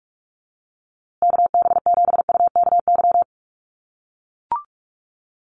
Morse code, heard as a keyed tone, about 44 wpm, 700 Hz. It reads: KB7UKY
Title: USA Call Signs Repeated 2x 45wpm